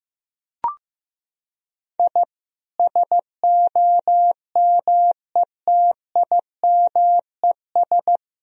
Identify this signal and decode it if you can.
Morse code, heard as a keyed tone, about 16 wpm, 700 Hz. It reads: I SOMETIMES
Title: Sentences from Top 1000 Words - Review Part 3 - 15wpm